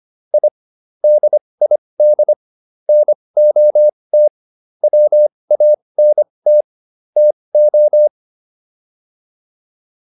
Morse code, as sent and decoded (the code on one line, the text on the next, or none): .. / -.. .. -.. / -. --- - / .-- .- -. - / - ---
I DID NOT WANT TO